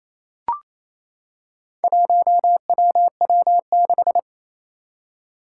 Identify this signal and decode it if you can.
Morse code, heard as a keyed tone, about 27 wpm, 700 Hz. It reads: 1WW6
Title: Four Challenging Letter-Number Combinations 28wpm